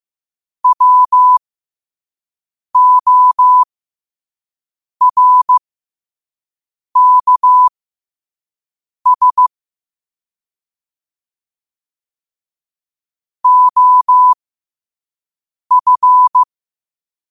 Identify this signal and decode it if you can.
Morse code, heard as a keyed tone, about 15 wpm, spaced as for 5 wpm, 1000 Hz. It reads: WORKS OF